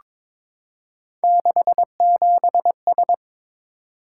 Morse code, heard as a keyed tone, about 22 wpm, 700 Hz. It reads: 67S
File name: practice / 22wpm / Three Letter-Number Combinations - 22wpm